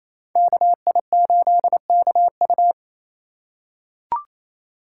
Morse code, heard as a keyed tone, about 28 wpm, 700 Hz. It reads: KI8KU
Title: USA Call Signs - 5 Characters - 2x Character Spacing - 28wpm